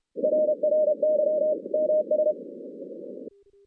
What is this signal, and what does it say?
Morse code, heard as a keyed tone, about 32 wpm, 610 Hz. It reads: RRY MS
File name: 2016 JAN CW NAQP / 2016 Jan naqp k5ot - 0-10